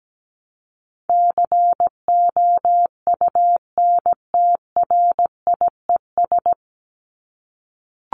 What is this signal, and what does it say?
Morse code, heard as a keyed tone, about 17 wpm, 700 Hz. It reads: COUNTRIES
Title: ES:S Suffix - Top 100 Words with Base Words 17wpm